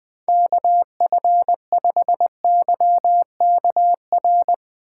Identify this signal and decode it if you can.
Morse code, heard as a keyed tone, about 20 wpm, 700 Hz. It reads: KF5YKR
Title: Speed Racing - USA 2x3 Call Signs 20wpm